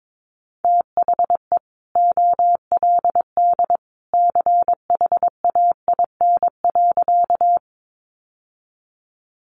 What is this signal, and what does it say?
Morse code, heard as a keyed tone, about 22 wpm, 700 Hz. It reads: THE OLD CHAIN.
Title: Sentences from Top 1500 Words 22wpm